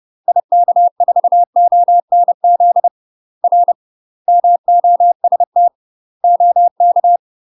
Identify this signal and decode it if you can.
Morse code, heard as a keyed tone, about 30 wpm, 700 Hz. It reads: IK4ONZ R MOST OK
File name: UK MASTER QSOs TRAINING / 30WPM / UK MASTER QSOs 0046